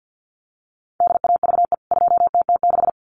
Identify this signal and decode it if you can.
Morse code, heard as a keyed tone, about 50 wpm, 700 Hz. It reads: DAVE 2TT6
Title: CWT Exchange 50wpm